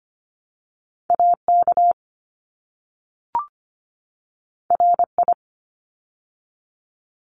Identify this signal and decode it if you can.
Morse code, heard as a keyed tone, about 25 wpm, 700 Hz. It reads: AK RI
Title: ARRL Sections 25wpm